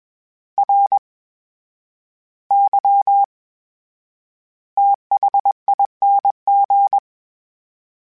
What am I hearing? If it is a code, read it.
Morse code, heard as a keyed tone, about 21 wpm, spaced as for 11 wpm, 800 Hz. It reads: RYTHING
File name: Special INGs-mp3s / Special INGs-mp3s / VerySlowToFaster Ing @20